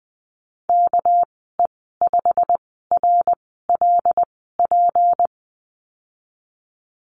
Morse code, heard as a keyed tone, about 19 wpm, spaced as for 14 wpm, 700 Hz. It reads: KE5RLP
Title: 219 Lesson 35g - Number 0 - Call Signs 20wpm